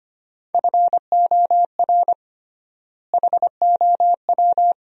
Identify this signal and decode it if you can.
Morse code, heard as a keyed tone, about 25 wpm, 700 Hz. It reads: FOR HOW
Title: 3x Word-Spacing - Sentences from Top 200 Words - 25wpm